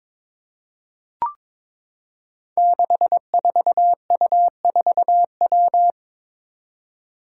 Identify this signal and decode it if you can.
Morse code, heard as a keyed tone, about 22 wpm, 700 Hz. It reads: 64U4W
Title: Five Challenging Letter-Number Combinations 22wpm